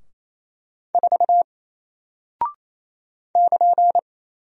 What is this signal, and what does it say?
Morse code, heard as a keyed tone, about 28 wpm, 700 Hz. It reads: 4 <KN>
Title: Single Character (Letter:Number:Punctuation:Prosign) Alt 28wpm